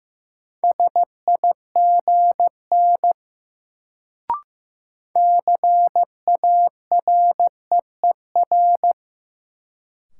Morse code, heard as a keyed tone, about 15 wpm, 700 Hz. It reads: SIGN CAREER
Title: Sentences from Top 1000 Words - New Word Drill - 15wpm